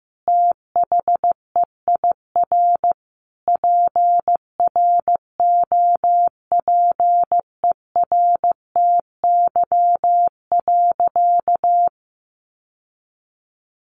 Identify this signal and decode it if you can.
Morse code, heard as a keyed tone, about 15 wpm, 700 Hz. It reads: THEIR PROPERTY.
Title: Sentences from Top 1000 Words 15wpm